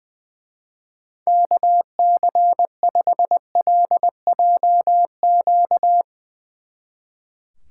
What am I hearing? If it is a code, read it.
Morse code, heard as a keyed tone, about 21 wpm, 700 Hz. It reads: KC5LJQ
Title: Speed Racing - USA 2x3 Call Signs 20wpm